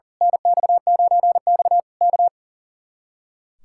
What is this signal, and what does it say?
Morse code, heard as a keyed tone, about 38 wpm, 700 Hz. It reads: NX9X K
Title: Speed-Racing Common QSO Phrases - 40wpm